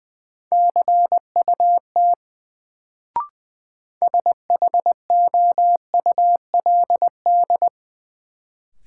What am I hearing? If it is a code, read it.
Morse code, heard as a keyed tone, about 20 wpm, 700 Hz. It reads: CUT SHOULD
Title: Sentences from Top 300 Words - New Word Drill - 20wpm